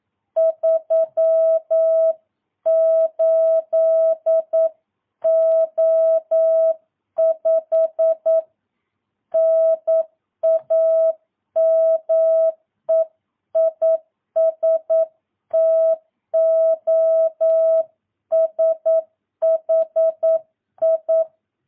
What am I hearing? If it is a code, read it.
Morse code, heard as a keyed tone, about 9 wpm, 650 Hz. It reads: 38O5 NAME IS TOSHI
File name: jh8cbh / 2022cw